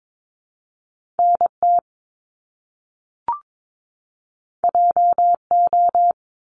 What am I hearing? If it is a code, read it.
Morse code, heard as a keyed tone, about 22 wpm, 700 Hz. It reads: NT JO
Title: Top Two Letter Combinations in Words 22wpm